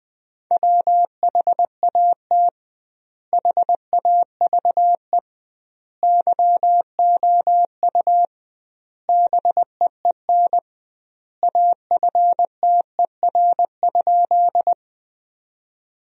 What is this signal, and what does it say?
Morse code, heard as a keyed tone, about 20 wpm, 700 Hz. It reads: WHAT HAVE YOU BEEN AFTER?
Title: 2x Word-Spacing - Sentences from Top 200 Words - 20wpm